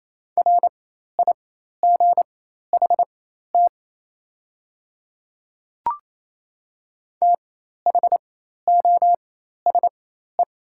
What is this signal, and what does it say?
Morse code, heard as a keyed tone, about 28 wpm, spaced as for 13 wpm, 700 Hz. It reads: RIGHT THOSE
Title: 4x Character Spacing - Top 300 Words - 28wpm